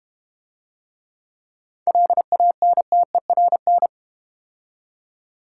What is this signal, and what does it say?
Morse code, heard as a keyed tone, about 32 wpm, 700 Hz. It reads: LANTERN